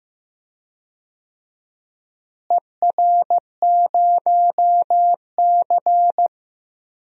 Morse code, heard as a keyed tone, about 15 wpm, 700 Hz. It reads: ER0C
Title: DXCC-100 Call Signs - 4 Characters - Repeated 3x - 15wpm